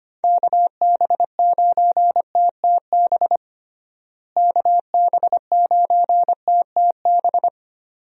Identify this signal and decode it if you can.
Morse code, heard as a keyed tone, about 25 wpm, 700 Hz. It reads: KB9TTB KB9TTB
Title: USA Call Signs - 6 Characters - Repeated 2x - 25wpm